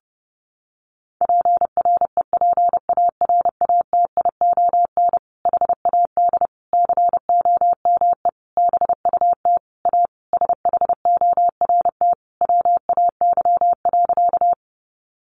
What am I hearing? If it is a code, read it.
Morse code, heard as a keyed tone, about 30 wpm, 700 Hz. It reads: PREPARATION HAD COME BUT A SHORT WAY.
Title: Sentences from Top 1400 Words 30wpm